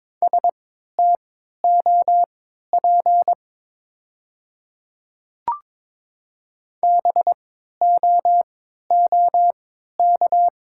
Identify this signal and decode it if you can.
Morse code, heard as a keyed tone, about 22 wpm, spaced as for 12 wpm, 700 Hz. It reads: STOP BOOK
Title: 3x Character Spacing - Top 300 Words - 22wpm